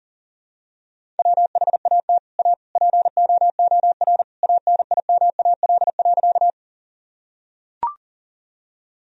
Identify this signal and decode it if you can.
Morse code, heard as a keyed tone, about 40 wpm, 700 Hz. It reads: WHAT A POOR ANIMAL.
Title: Sentences from Top 1000 Words - Review Part 1 - 40wpm